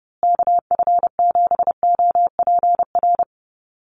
Morse code, heard as a keyed tone, about 30 wpm, 700 Hz. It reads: KF7OPR
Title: USA Call Signs Repeated 2x 30wpm